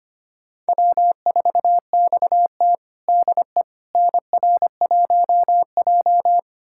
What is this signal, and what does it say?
Morse code, heard as a keyed tone, about 25 wpm, 700 Hz. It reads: W4XT DE NR1J
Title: USA Callsign DE Callsign - 4 Characters - 25wpm